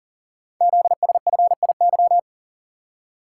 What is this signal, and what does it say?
Morse code, heard as a keyed tone, about 43 wpm, 700 Hz. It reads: ZSFIY
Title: Code Groups - 5 Letters - 40wpm